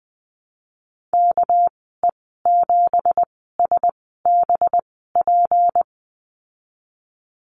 Morse code, heard as a keyed tone, about 20 wpm, spaced as for 15 wpm, 700 Hz. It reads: KE7SBP
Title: 207 Lesson 33g - Letter j - Call Signs 20wpm